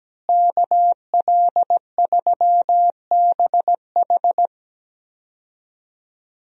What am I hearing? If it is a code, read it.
Morse code, heard as a keyed tone, about 17 wpm, 700 Hz. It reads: KL3BH
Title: USA Call Signs - 5 Characters - Repeated 3x - 17wpm